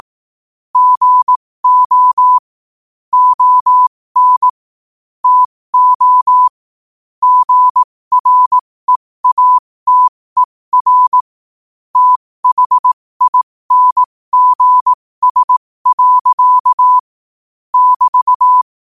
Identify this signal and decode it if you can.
Morse code, heard as a keyed tone, about 18 wpm, spaced as for 15 wpm, 1000 Hz. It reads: GO ON TO GREATER THINGS. <BT>
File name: qotdfiles / 15wpm